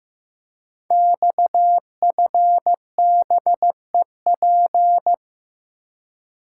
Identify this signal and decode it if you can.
Morse code, heard as a keyed tone, about 15 wpm, 700 Hz. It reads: XFBEP
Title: Code Groups - 5 Letter:Numbers - 15wpm